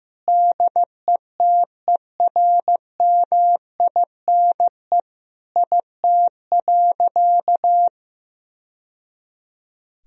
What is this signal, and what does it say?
Morse code, heard as a keyed tone, about 15 wpm, 700 Hz. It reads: DETERMINE IT.
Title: Sentences from Top 1000 Words - Review Part 4 - 15wpm